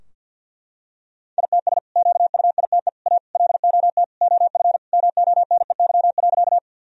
Morse code, heard as a keyed tone, about 47 wpm, 700 Hz. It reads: ITS QUITE A LOT OF MONEY.
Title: Sentences from Top 1000 Words - Review Part 3 - 50wpm